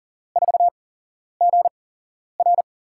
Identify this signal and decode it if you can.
Morse code, heard as a keyed tone, about 40 wpm, spaced as for 11 wpm, 700 Hz. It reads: 4GR
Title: USA Call Signs - 5 Characters - 8x Character Spacing - 40wpm